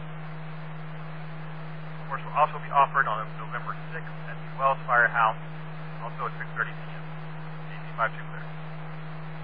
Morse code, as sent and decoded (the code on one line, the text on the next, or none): none